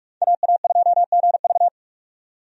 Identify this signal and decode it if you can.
Morse code, heard as a keyed tone, about 41 wpm, 700 Hz. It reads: AA2GV
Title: USA Call Signs - 5 Characters - 45wpm